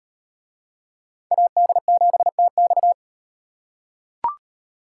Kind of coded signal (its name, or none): Morse code